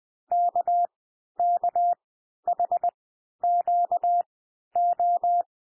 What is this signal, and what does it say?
Morse code, heard as a keyed tone, about 20 wpm, spaced as for 11 wpm, 700 Hz. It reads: KKHQO
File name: cwmp3 / L1 SOUND LETTERS 20 wpm